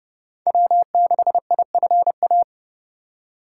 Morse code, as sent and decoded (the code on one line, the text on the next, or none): .-- -.... .. ..-. .-
W6IFA